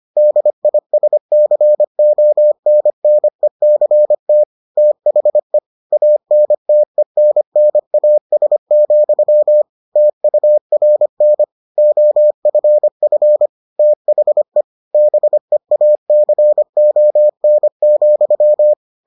Morse code, heard as a keyed tone, about 25 wpm, 600 Hz. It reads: DISCONNECT THE ANTENNAS, TURN OFF THE BEACON,